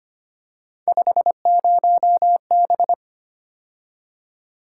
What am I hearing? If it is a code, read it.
Morse code, heard as a keyed tone, about 29 wpm, 700 Hz. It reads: 50B
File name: Speed-Racing - Three Letter-Number Combinations - 25wpm